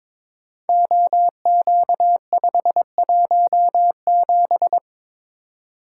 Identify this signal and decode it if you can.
Morse code, heard as a keyed tone, about 22 wpm, 700 Hz. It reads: OQ517
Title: Code Groups - 5 Letter:Numbers - 22wpm